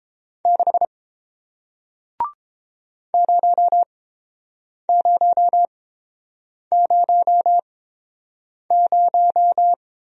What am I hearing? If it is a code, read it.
Morse code, heard as a keyed tone, about 30 wpm, spaced as for 7 wpm, 700 Hz. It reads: 6 0000